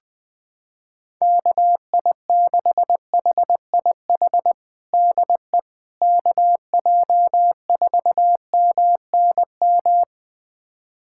Morse code, heard as a keyed tone, about 21 wpm, 700 Hz. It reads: KI6HIH DE KJ4MNM
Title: USA Callsign DE Callsign 20wpm